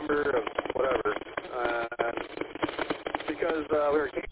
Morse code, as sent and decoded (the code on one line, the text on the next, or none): none